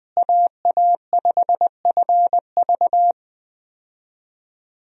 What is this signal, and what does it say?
Morse code, heard as a keyed tone, about 20 wpm, 700 Hz. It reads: AA5FV
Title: USA Call Signs - 5 Characters - Repeated 2x - 20wpm